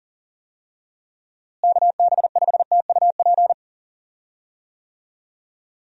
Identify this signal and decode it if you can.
Morse code, heard as a keyed tone, about 40 wpm, 700 Hz. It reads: KB5TUP
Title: USA Call Signs - 6 Characters - Repeated 2x - 40wpm